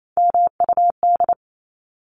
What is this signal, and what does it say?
Morse code, heard as a keyed tone, about 28 wpm, 700 Hz. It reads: MUD
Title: Top 1751-2000 Words 28wpm